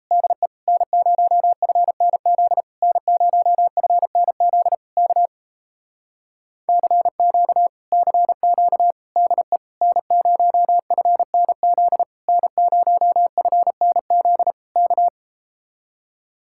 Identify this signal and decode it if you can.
Morse code, heard as a keyed tone, about 38 wpm, 700 Hz. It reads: DE N0FNZ N0FNZ K CQ CQ DE N0FNZ N0FNZ K